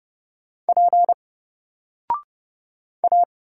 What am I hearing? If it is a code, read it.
Morse code, heard as a keyed tone, about 30 wpm, 700 Hz. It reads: P A
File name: Single-Letter-Number-30wpm